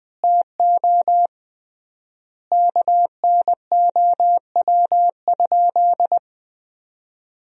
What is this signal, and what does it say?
Morse code, heard as a keyed tone, about 20 wpm, 700 Hz. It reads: TO KNOW?
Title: 3x Word-Spacing - Sentences from Top 200 Words - 20wpm